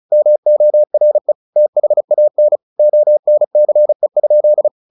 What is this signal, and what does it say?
Morse code, heard as a keyed tone, about 34 wpm, 600 Hz. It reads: MORE THAN ONCE?
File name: the old man and the sea / the old man and the sea 1 35wpm